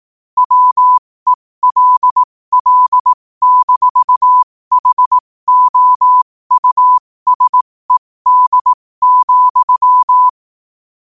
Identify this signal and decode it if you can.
Morse code, heard as a keyed tone, about 18 wpm, spaced as for 15 wpm, 1000 Hz. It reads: WELL-HOUSED,
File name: qotdfiles / 15wpm